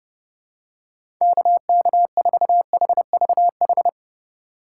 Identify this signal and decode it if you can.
Morse code, heard as a keyed tone, about 32 wpm, 700 Hz. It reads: KK4HVH
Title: USA Call Signs - 6 Characters - 4x Character Spacing - 30wpm